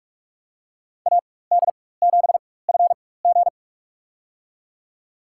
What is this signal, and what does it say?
Morse code, heard as a keyed tone, about 46 wpm, spaced as for 21 wpm, 700 Hz. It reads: AD7FG
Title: USA Call Signs - 4x Character Spacing - 45wpm